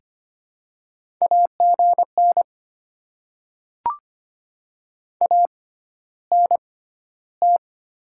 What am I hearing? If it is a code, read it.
Morse code, heard as a keyed tone, about 25 wpm, spaced as for 13 wpm, 700 Hz. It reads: AGN ANT
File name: Top 20 QSO Elements - 6x Character Spacing - 25wpm